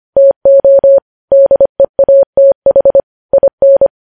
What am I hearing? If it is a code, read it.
Morse code, heard as a keyed tone, about 25 wpm, 550 Hz. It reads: TO DEATH IN
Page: news headlines in morse code at 25 wpm